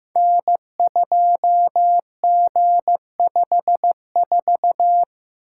Morse code, heard as a keyed tone, about 15 wpm, 700 Hz. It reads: N2G54
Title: Code Groups - 5 Letter:Numbers - 15wpm